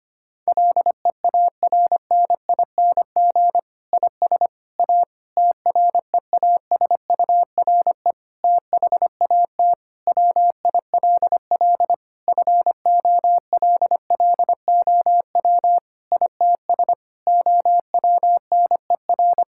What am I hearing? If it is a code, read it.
Morse code, heard as a keyed tone, about 25 wpm, 700 Hz. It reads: LEARNING IS A TREASURE THAT WILL FOLLOW ITS OWNER
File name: practice / 25wpm / Difficult Sentences with Hints - 25wpm